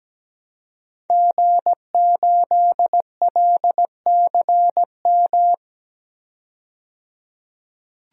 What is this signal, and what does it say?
Morse code, heard as a keyed tone, about 17 wpm, 700 Hz. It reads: G8LCM
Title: DXCC-100 Call Signs - 5 Characters - 17wpm